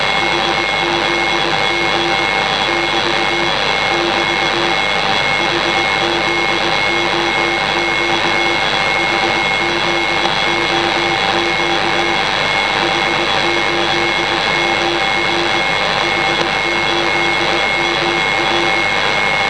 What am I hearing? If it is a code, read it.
Morse code, heard as a keyed tone, about 20 wpm, 340 Hz, with noise. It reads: HZG <BT> <BT> HZOQ HZOQ H8MR H8MR
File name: HZH Low Frequency